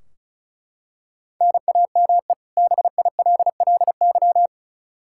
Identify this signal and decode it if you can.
Morse code, heard as a keyed tone, about 36 wpm, 700 Hz. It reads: NAME BILLY
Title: Practice QSO - Exchange 1 - Beginning - 35wpm